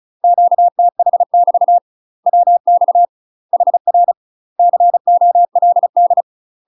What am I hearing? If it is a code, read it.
Morse code, heard as a keyed tone, about 36 wpm, 700 Hz. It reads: QTH <BT> WX HR COLD
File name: UK MASTER QSOs TRAINING / 35WPM / UK MASTER QSOs 0012